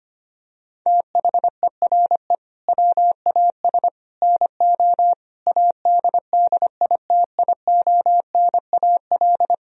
Morse code, heard as a keyed tone, about 25 wpm, 700 Hz. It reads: THERE WAS NO ADDITIONAL
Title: Sentences from Top 1000 Words - Review Part 4 - 25wpm